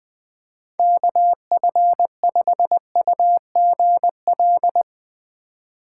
Morse code, heard as a keyed tone, about 20 wpm, 700 Hz. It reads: KF5UGL
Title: USA Call Signs - 4x Character Spacing - 20wpm